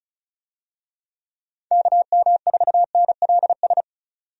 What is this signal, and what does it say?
Morse code, heard as a keyed tone, about 34 wpm, 700 Hz. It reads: KM4NLS
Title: USA Call Signs - 6 Characters - 35wpm